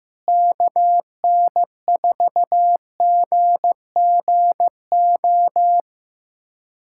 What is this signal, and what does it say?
Morse code, heard as a keyed tone, about 16 wpm, 700 Hz. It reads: KN4GGO
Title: USA Callsign DE Callsign - 6 Characters - 15wpm